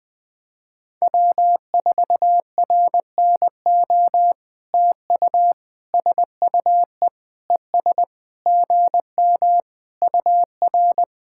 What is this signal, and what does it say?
Morse code, heard as a keyed tone, about 20 wpm, 700 Hz. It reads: W4RNO TU SUE ES GM UR
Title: POTA QSO Practice - 20wpm